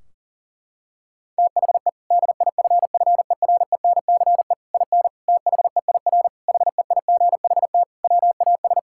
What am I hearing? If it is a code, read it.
Morse code, heard as a keyed tone, about 37 wpm, 700 Hz. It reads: THE DIFFERENCE IN THEIR HEIGHT WAS
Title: Sentences from Top 1600 Words 40wpm